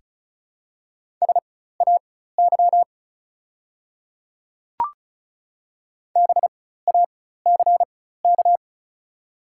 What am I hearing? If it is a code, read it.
Morse code, heard as a keyed tone, about 34 wpm, spaced as for 16 wpm, 700 Hz. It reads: SAY BACK
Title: 4x Character Spacing - Top 300 Words - 35wpm